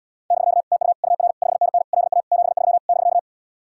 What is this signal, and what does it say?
Morse code, heard as a keyed tone, about 79 wpm, 700 Hz. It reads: <BK> TU UR 5NN LA 73 <BK>